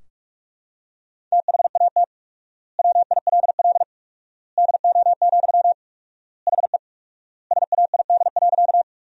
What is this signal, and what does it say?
Morse code, heard as a keyed tone, about 42 wpm, 700 Hz. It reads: THAT WILL DO, HE SAID.
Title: 4x Word-Spacing - Sentences from Top 100 Words - 45wpm